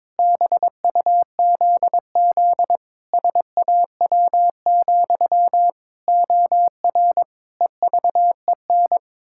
Morse code, heard as a keyed tone, about 21 wpm, 700 Hz. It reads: BUZZ SAW, OR EVEN
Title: Pangram Challenge v2 - 22wpm